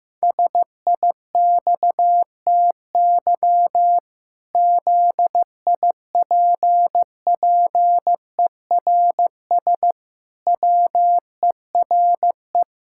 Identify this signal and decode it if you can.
Morse code, heard as a keyed tone, about 15 wpm, 700 Hz. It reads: SIXTY ZIPPERS WERE
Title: Pangram Challenge v2 - 15wpm